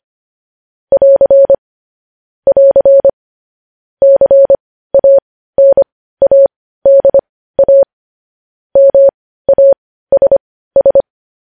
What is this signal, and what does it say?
Morse code, heard as a keyed tone, about 25 wpm, spaced as for 15 wpm, 550 Hz. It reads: <AR> <AR> CANADA MASS
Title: News Headlines at 15